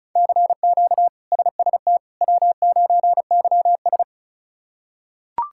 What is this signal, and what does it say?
Morse code, heard as a keyed tone, about 34 wpm, 700 Hz. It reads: CQ SST W9YS
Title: K1USN SST QSO Practice - 35wpm